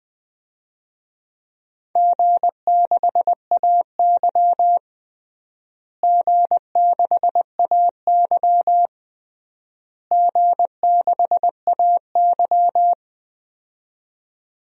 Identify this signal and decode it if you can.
Morse code, heard as a keyed tone, about 20 wpm, 700 Hz. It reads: G6AY G6AY G6AY
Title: DXCC-100 Call Signs - 4 Characters - Repeated 3x - 20wpm